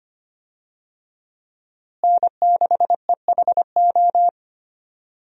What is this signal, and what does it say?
Morse code, heard as a keyed tone, about 25 wpm, 700 Hz. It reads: N6EHO
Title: USA Call Signs - 5 Characters - Repeated 2x - 25wpm